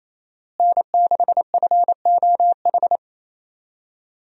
Morse code, heard as a keyed tone, about 28 wpm, 700 Hz. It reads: N6FOH
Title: USA Call Signs - 5 Characters - Repeated 3x - 28wpm